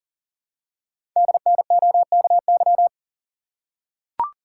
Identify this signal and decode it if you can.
Morse code, heard as a keyed tone, about 42 wpm, 700 Hz. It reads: DNOKY
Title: Code Groups - 5 Letters - 40wpm